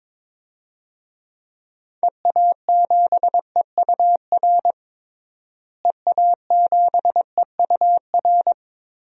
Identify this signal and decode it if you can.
Morse code, heard as a keyed tone, about 22 wpm, 700 Hz. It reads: EA7EUR EA7EUR
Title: DXCC-100 Call Signs - 6 Characters - Repeated 2x - 22wpm